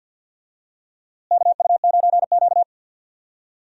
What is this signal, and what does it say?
Morse code, heard as a keyed tone, about 49 wpm, 700 Hz. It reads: KU9Q
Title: USA Call Signs - 4 Characters - Repeated 2x - 50wpm